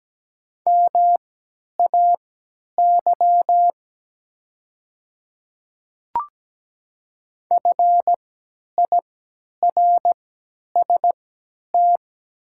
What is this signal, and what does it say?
Morse code, heard as a keyed tone, about 17 wpm, spaced as for 10 wpm, 700 Hz. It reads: MAY FIRST
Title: 3x Character Spacing - Top 100 Words - 17wpm